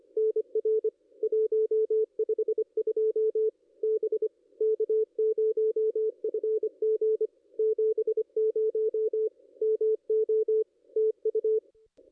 Morse code, heard as a keyed tone, about 27 wpm, 440 Hz. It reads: NR 152 B K0FG 70 MO TU
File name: contests / ss-cw-k5tr-op-n5rz -0400